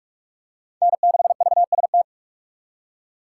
Morse code, heard as a keyed tone, about 42 wpm, 700 Hz. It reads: N6VST